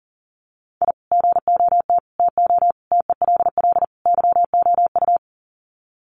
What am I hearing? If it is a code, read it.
Morse code, heard as a keyed tone, about 40 wpm, 700 Hz. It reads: I GOT TO TELL YOU
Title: Sets of 5 Words Encore2 40wpm